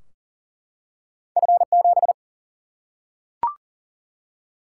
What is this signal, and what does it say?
Morse code, heard as a keyed tone, about 40 wpm, 700 Hz. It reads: F7
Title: Single-Letter Single-Number 40wpm